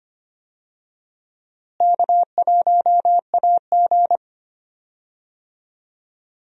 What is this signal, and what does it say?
Morse code, heard as a keyed tone, about 25 wpm, 700 Hz. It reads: K1AG
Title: USA Call Signs - 4 Characters - Repeated 3x - 25wpm